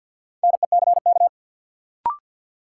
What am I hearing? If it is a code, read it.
Morse code, heard as a keyed tone, about 47 wpm, 700 Hz. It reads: NECK
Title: Sets of 5 Words Encore - New Word Drill - 50wpm